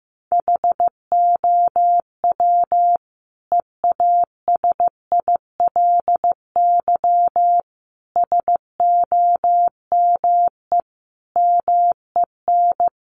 Morse code, heard as a keyed tone, about 15 wpm, 700 Hz. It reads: HOW EASILY SOME MEN
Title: Sentences from Top 900 Words 15wpm